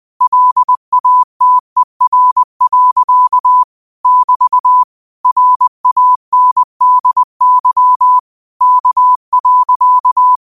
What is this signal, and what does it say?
Morse code, heard as a keyed tone, about 20 wpm, 1000 Hz. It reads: LATER. <BT> RANDY K.
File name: qotdfiles / 20wpm